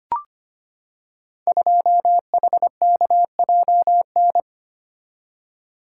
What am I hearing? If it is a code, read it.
Morse code, heard as a keyed tone, about 25 wpm, 700 Hz. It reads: 2HKJN
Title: Code Groups - 5 Letter:Numbers - 25wpm